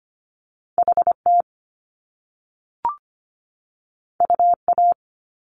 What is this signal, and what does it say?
Morse code, heard as a keyed tone, about 25 wpm, 700 Hz. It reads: HT UA
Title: Top Two Letter Combinations in Words 25wpm